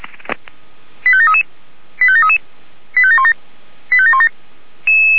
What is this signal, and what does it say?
Five-tone selective call (ZVEI), heard as quick runs of five tones, tones of about 70 ms. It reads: 85733 85733 85717 85717
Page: SDR/ZVEI